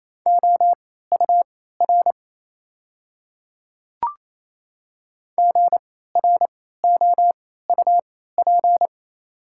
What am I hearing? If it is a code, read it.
Morse code, heard as a keyed tone, about 28 wpm, spaced as for 16 wpm, 690 Hz. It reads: OUR GROUP
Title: 3x Character Spacing - Top 300 Words - 28wpm